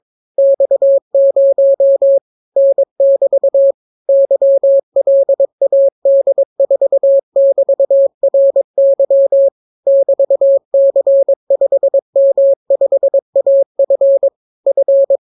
Morse code, heard as a keyed tone, about 22 wpm, 550 Hz. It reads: X0 N<BT> YLAD4<BT>RY <BT>C5M5AF F